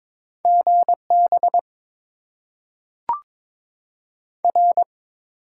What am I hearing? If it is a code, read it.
Morse code, heard as a keyed tone, about 22 wpm, 700 Hz. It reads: GB R
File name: Top 90 QSO Elements - 8x Character Spacing - 22wpm